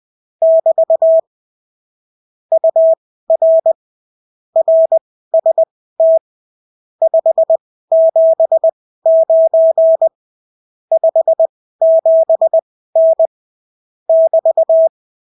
Morse code, heard as a keyed tone, about 20 wpm, spaced as for 14 wpm, 650 Hz. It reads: <BT> UR RST 579 57N <BT>
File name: CW QSO libroCW4 20-10ppm